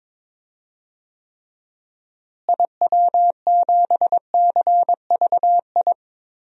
Morse code, heard as a keyed tone, about 22 wpm, 700 Hz. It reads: IW7CVI
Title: DXCC-100 Call Signs - 6 Characters - Repeated 3x - 22wpm